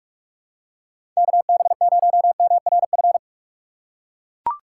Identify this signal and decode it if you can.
Morse code, heard as a keyed tone, about 43 wpm, 700 Hz. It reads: KB0MRF
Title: USA Call Signs - 6 Characters - 3x Character Spacing - 45wpm